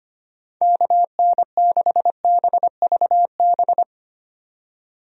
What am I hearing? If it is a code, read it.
Morse code, heard as a keyed tone, about 25 wpm, 700 Hz. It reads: KN6BVB
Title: USA Call Signs - 6 Characters - 25wpm